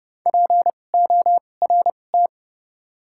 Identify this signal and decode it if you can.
Morse code, heard as a keyed tone, about 30 wpm, spaced as for 22 wpm, 700 Hz. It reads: PORT
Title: 2x Character Spacing - Top 500 Words - 30wpm